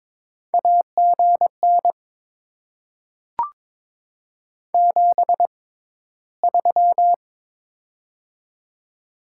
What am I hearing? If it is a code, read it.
Morse code, heard as a keyed tone, about 22 wpm, spaced as for 14 wpm, 700 Hz. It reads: AGN 73
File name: Top 20 QSO Elements - 6x Character Spacing - 22wpm